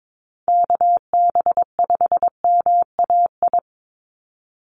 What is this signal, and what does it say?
Morse code, heard as a keyed tone, about 22 wpm, 700 Hz. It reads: KB5MAI
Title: USA Call Signs 22wpm